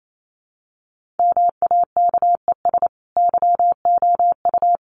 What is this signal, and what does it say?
Morse code, heard as a keyed tone, about 28 wpm, 700 Hz. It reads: MAKES YOU
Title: A Sound Of Thunder 28wpm